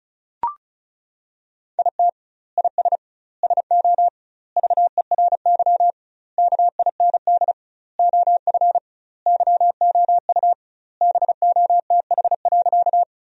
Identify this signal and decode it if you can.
Morse code, heard as a keyed tone, about 36 wpm, 700 Hz. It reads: IT IS SO VERY KIND OF YOU BOTH.
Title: 2x Word-Spacing - Sentences from Top 300 Words - 35wpm